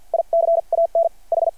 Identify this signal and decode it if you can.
Morse code, heard as a keyed tone, about 52 wpm, spaced as for 42 wpm, 660 Hz. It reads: I CAN H
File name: archived / CRSnet